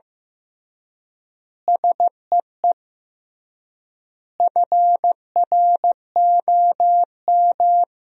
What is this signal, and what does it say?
Morse code, heard as a keyed tone, about 15 wpm, 700 Hz. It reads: SEE FROM
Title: 3x Word-Spacing - Sentences from Top 100 Words - 15wpm